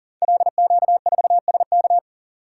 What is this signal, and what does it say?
Morse code, heard as a keyed tone, about 39 wpm, 700 Hz. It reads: LQ4SK
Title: Code Groups - 5 Letter:Numbers - 40wpm